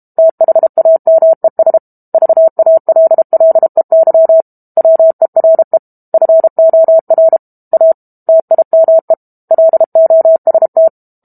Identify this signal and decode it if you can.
Morse code, heard as a keyed tone, about 32 wpm, 650 Hz. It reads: THAMES VALLEY WERE FOR A TIME LOST
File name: Book Two / 06-Chapter 6 - 32 WPM